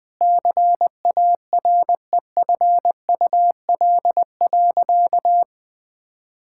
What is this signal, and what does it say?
Morse code, heard as a keyed tone, about 21 wpm, 700 Hz. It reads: CAREFUL.
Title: Sentences from Top 1600 Words 20wpm